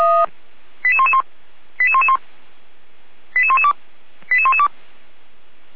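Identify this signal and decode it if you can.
Five-tone selective call (ZVEI), heard as quick runs of five tones, tones of about 70 ms. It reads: 88111 88111 88112 88112
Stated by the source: Received on an SDR, ZVEI five-tone call 88497, from the feuerwehr-orlamuende.org audio archive (Feuerwehr Orlamünde).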